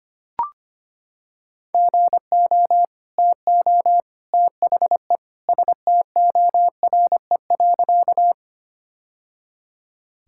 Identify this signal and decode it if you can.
Morse code, heard as a keyed tone, about 25 wpm, 700 Hz. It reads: GO TO THE STORE.
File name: practice / 25wpm / Sentences with 2-Word Phrases - 25wpm